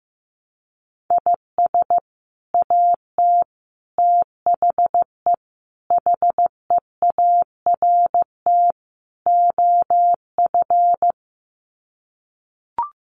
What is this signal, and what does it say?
Morse code, heard as a keyed tone, about 15 wpm, 700 Hz. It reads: IS AT THE HEART OF
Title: Sets of 5 Words Encore2 15wpm